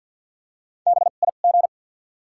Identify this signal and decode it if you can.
Morse code, heard as a keyed tone, about 53 wpm, 700 Hz. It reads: B I G
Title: 2x Character Spacing - Top 300 Words - 50wpm